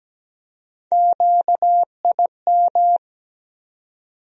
Morse code, heard as a keyed tone, about 17 wpm, 700 Hz. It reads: QIM
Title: USA Call Sign Postfixes - 3 Letters 17wpm